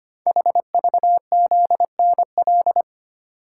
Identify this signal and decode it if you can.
Morse code, heard as a keyed tone, about 25 wpm, 700 Hz. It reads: HVZNL
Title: Code Groups - 5 Letters - 25wpm